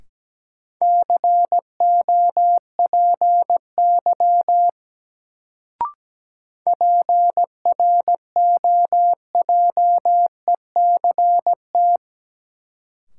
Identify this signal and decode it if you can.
Morse code, heard as a keyed tone, about 17 wpm, 700 Hz. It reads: COPY PROJECT
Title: Sentences from Top 1400 Words - New Word Drill - 17wpm